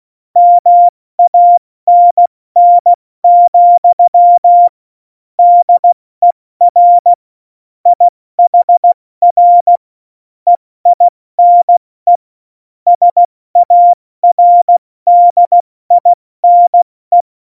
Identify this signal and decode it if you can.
Morse code, heard as a keyed tone, about 16 wpm, spaced as for 14 wpm, 700 Hz. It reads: MANN, DER IHR EINE SARDINE
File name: Morsekurs / Witze / Witz 16. Januar 2018, 70 F80